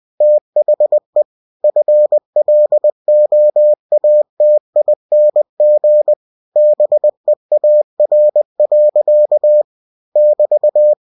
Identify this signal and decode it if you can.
Morse code, heard as a keyed tone, about 20 wpm, 600 Hz. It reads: THE FLOATING BEAR. <BT>
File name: cw-books / winnie-the-pooh / winnie-20wpm-page-0099